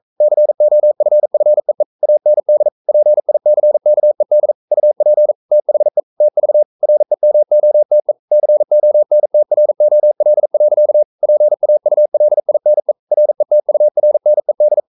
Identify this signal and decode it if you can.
Morse code, heard as a keyed tone, about 42 wpm, 600 Hz. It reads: COFFEE AND PICKED UP THE TV REMOTE CONTROL. PAULINE RETURNED
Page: Romanen Never af Ken Follet på engelsk.